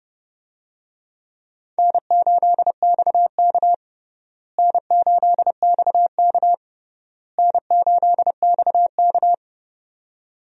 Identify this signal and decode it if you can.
Morse code, heard as a keyed tone, about 30 wpm, 700 Hz. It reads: N8XK N8XK N8XK
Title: USA Call Signs - 4 Characters - Repeated 3x - 30wpm